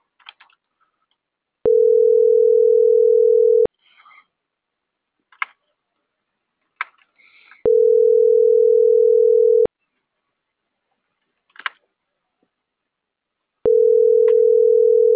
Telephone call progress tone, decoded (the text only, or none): ringback tone